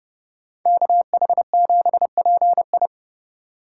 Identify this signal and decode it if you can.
Morse code, heard as a keyed tone, about 29 wpm, 700 Hz. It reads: KH7PI